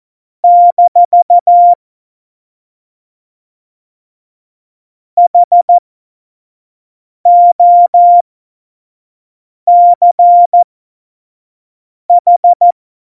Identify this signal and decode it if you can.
Morse code, heard as a keyed tone, about 14 wpm, spaced as for 5 wpm, 700 Hz. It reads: - HOCH
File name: Morsekurs / Witze / Witz 12. Dezember 2017, 25 F70